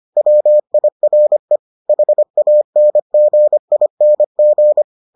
Morse code, heard as a keyed tone, about 25 wpm, 600 Hz. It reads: WIRE HANGING